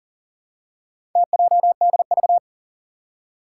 Morse code, heard as a keyed tone, about 40 wpm, 700 Hz. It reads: TJDV
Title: Four Challenging Letter-Number Combinations 40wpm